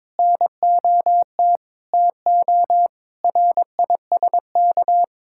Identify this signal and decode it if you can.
Morse code, heard as a keyed tone, about 22 wpm, 700 Hz. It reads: NOT TO RISK
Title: Sentences from Top 1600 Words 22wpm